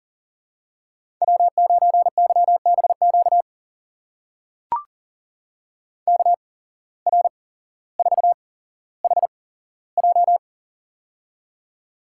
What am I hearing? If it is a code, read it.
Morse code, heard as a keyed tone, about 39 wpm, spaced as for 17 wpm, 700 Hz. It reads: W9YBQ KR4HJ
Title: USA Call Signs - 5 Characters - 8x Character Spacing - 40wpm